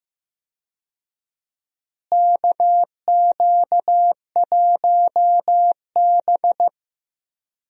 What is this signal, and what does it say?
Morse code, heard as a keyed tone, about 15 wpm, 700 Hz. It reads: KQ1B
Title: USA Call Signs - 4 Characters - Repeated 2x - 15wpm